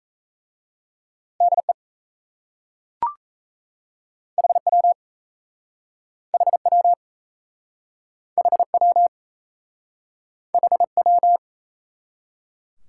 Morse code, heard as a keyed tone, about 32 wpm, 700 Hz. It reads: DE HW HW HW HW
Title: Speed-Racing Top 90 QSO Elements 28wpm